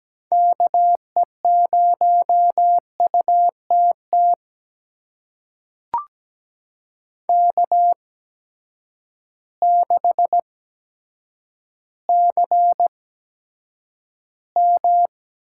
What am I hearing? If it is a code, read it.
Morse code, heard as a keyed tone, about 17 wpm, spaced as for 8 wpm, 700 Hz. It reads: KE0UTT K6CM
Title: USA Call Signs - 8x Character Spacing - 17wpm